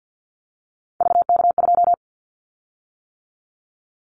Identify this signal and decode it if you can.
Morse code, heard as a keyed tone, about 48 wpm, 700 Hz. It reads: VK2
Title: Callsign Prefixes for Top 100 DXCC - 50wpm